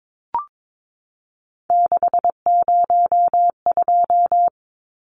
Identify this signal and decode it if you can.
Morse code, heard as a keyed tone, about 22 wpm, 700 Hz. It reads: 602
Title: Numbers - 3-Digit 22wpm